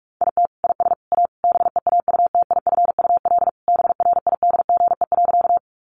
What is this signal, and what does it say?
Morse code, heard as a keyed tone, about 44 wpm, 700 Hz. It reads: IT IS A BEAUTIFUL BRIDGE.
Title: Sentences from Top 1200 Words 45wpm